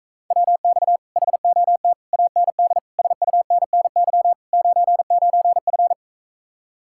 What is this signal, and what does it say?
Morse code, heard as a keyed tone, about 45 wpm, 700 Hz. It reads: WX HOT AND SUNNY 99F